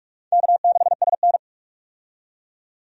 Morse code, heard as a keyed tone, about 42 wpm, 700 Hz. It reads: K6SN
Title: USA Call Signs - 4 Characters - 45wpm